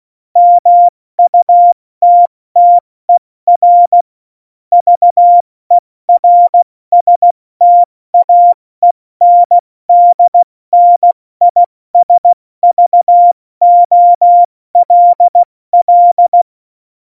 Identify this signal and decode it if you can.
Morse code, heard as a keyed tone, about 16 wpm, 700 Hz. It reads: MUTTER VERSTAENDNISVOLL
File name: Morsekurs / Witze / Witz 25. Juni 2019, 70 F80